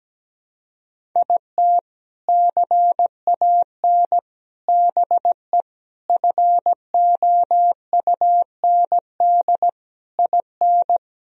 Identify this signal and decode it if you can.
Morse code, heard as a keyed tone, about 17 wpm, 700 Hz. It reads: IT CAN BE FOUND IN
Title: Sentences from Top 1000 Words - Review Part 1 - 17wpm